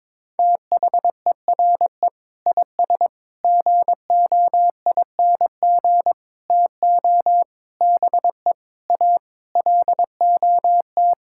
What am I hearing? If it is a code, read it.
Morse code, heard as a keyed tone, about 22 wpm, 700 Hz. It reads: THERE IS GOING TO BE A LOT
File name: Sentences with 7-Word Phrases - 22wpm